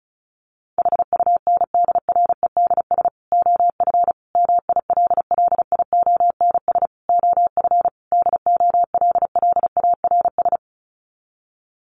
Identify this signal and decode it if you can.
Morse code, heard as a keyed tone, about 34 wpm, 700 Hz. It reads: HUNDREDS OF MILLIONS OF DOLLARS
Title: Sets of 5 Words 35wpm